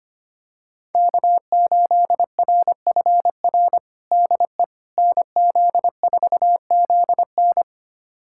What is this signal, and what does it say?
Morse code, heard as a keyed tone, about 25 wpm, 700 Hz. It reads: K8RFR DE NZ4ZN
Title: USA Callsign DE Callsign - 5 Characters - 25wpm